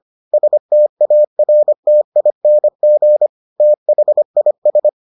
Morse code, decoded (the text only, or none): STARTING THIS